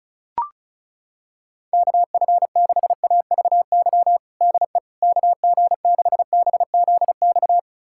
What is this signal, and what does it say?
Morse code, heard as a keyed tone, about 36 wpm, 700 Hz. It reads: KF6AVY DE KG6BZX
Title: USA Callsign DE Callsign - 6 Characters - 35wpm